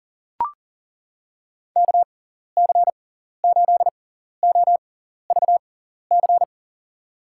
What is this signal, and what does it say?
Morse code, heard as a keyed tone, about 40 wpm, spaced as for 14 wpm, 700 Hz. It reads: KC8OVC
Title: USA Call Signs - 6 Characters - 6x Character Spacing - 40wpm